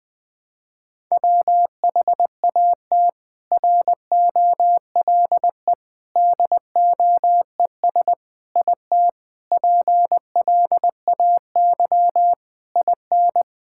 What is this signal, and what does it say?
Morse code, heard as a keyed tone, about 20 wpm, 700 Hz. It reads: WHAT ROLE DOES IT PLAY IN
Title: Sentences from Top 1000 Words - Review Part 6 - 20wpm